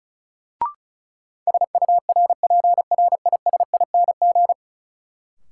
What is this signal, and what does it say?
Morse code, heard as a keyed tone, about 34 wpm, 700 Hz. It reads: SURPRISING
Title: Sets of 5 Words - New Word Drill - 35wpm